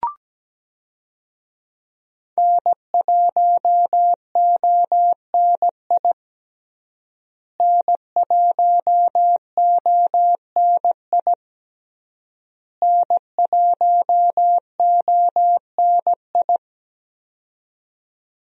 Morse code, heard as a keyed tone, about 17 wpm, 700 Hz. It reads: N1ONI N1ONI N1ONI
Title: USA Call Signs - 5 Characters - Repeated 3x - 17wpm